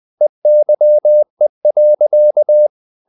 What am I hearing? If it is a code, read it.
Morse code, heard as a keyed tone, about 20 wpm, 600 Hz. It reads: EYE.
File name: the old man and the sea part 2 20wpm